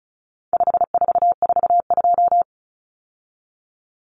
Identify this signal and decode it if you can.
Morse code, heard as a keyed tone, about 34 wpm, 700 Hz. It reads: 5442
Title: Numbers - 4-Digit 35wpm